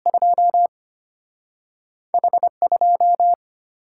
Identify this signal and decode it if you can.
Morse code, heard as a keyed tone, about 33 wpm, 700 Hz. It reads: 2 H2